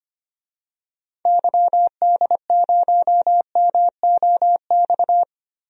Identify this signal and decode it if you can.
Morse code, heard as a keyed tone, about 25 wpm, 700 Hz. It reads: YD0MOX
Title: DXCC-100 Call Signs - 6 Characters - 25wpm